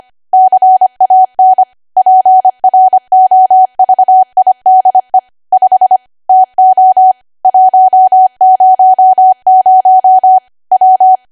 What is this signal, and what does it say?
Morse code, heard as a keyed tone, about 25 wpm, 750 Hz. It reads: CAN PROVIDE 5 TO 100 W
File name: Archive / 25 WPM / 25WPM